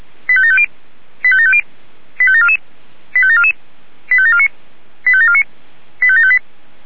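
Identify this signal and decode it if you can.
Five-tone selective call (ZVEI), heard as quick runs of five tones, tones of about 70 ms. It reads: 85750 85750 85744 85744 85749 85749 85758
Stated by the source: ZVEI five-tone selective call received with an SDR.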